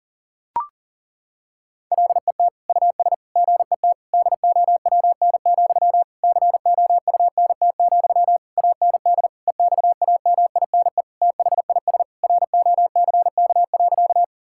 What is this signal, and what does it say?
Morse code, heard as a keyed tone, about 40 wpm, 700 Hz. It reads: LET US GET DOWN, COUNT, AND EXAMINE THIS ROCK.
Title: Sentences from Top 1600 Words 40wpm